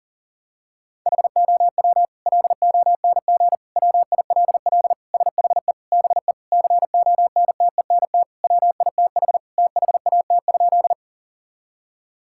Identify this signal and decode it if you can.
Morse code, heard as a keyed tone, about 40 wpm, 700 Hz. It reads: HOW LONG WILL SHE BE CONTENT WITH THAT?
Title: Sentences from Top 1000 Words - Review Part 6 - 40wpm